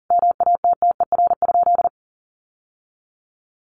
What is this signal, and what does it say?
Morse code, heard as a keyed tone, about 40 wpm, 700 Hz. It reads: MATTER?
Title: Sentences from Top 700 Words 40wpm